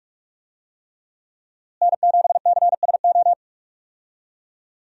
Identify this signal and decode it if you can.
Morse code, heard as a keyed tone, about 41 wpm, 700 Hz. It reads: N7CSO
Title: USA Call Signs - 5 Characters - Repeated 3x - 45wpm